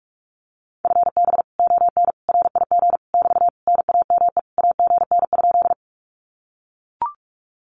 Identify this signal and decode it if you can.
Morse code, heard as a keyed tone, about 44 wpm, 700 Hz. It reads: FB ON RIG <BT> NAME AGN?
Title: Common QSO Phrases Encore 45wpm